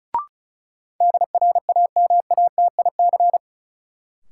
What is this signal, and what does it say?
Morse code, heard as a keyed tone, about 36 wpm, 700 Hz. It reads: DRAMATIC
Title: Sentences from Top 1300 Words - New Word Drill - 35wpm